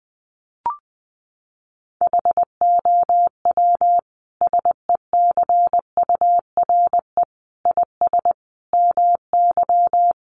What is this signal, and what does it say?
Morse code, heard as a keyed tone, about 20 wpm, 700 Hz. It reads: HOW SECURE IS MY
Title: Sentences from Top 1200 Words 20wpm